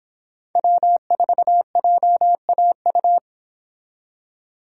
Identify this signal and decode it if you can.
Morse code, heard as a keyed tone, about 25 wpm, 700 Hz. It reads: W4JAU